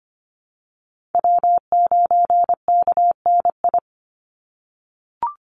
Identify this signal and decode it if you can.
Morse code, heard as a keyed tone, about 25 wpm, 700 Hz. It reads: W9KNI
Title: Call Signs - 25wpm